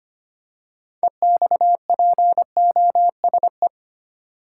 Morse code, heard as a keyed tone, about 25 wpm, 700 Hz. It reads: EXPOSE
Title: Six Letter Words - 25wpm